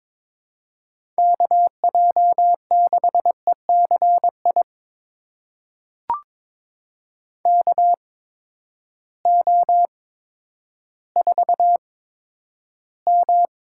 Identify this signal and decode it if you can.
Morse code, heard as a keyed tone, about 22 wpm, spaced as for 11 wpm, 700 Hz. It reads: KJ6ECI KO4M
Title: USA Call Signs - 8x Character Spacing - 22wpm